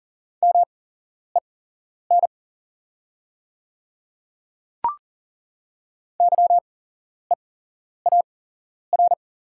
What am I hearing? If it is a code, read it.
Morse code, heard as a keyed tone, about 37 wpm, spaced as for 11 wpm, 700 Hz. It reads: MEN YEAR
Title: 8x Character Spacing - Top 200 Words - 40wpm